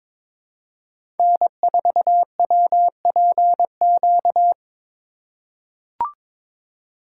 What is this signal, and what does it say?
Morse code, heard as a keyed tone, about 22 wpm, 700 Hz. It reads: N4WPQ
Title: USA Call Signs - 4x Character Spacing - 22wpm